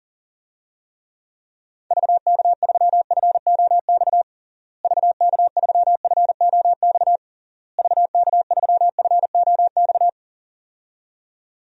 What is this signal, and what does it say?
Morse code, heard as a keyed tone, about 37 wpm, 700 Hz. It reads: VK3FOX VK3FOX VK3FOX
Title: DXCC-100 Call Signs - 6 Characters - Repeated 3x - 40wpm